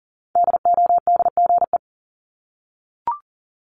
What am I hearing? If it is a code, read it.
Morse code, heard as a keyed tone, about 39 wpm, 700 Hz. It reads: DODGE
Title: Multi-letter Phonemes 40wpm